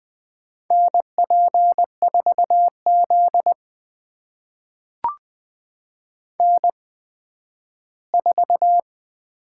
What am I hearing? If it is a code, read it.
Morse code, heard as a keyed tone, about 20 wpm, spaced as for 12 wpm, 700 Hz. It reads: NP4Z N4
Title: USA Call Signs - 4 Characters - 8x Character Spacing - 20wpm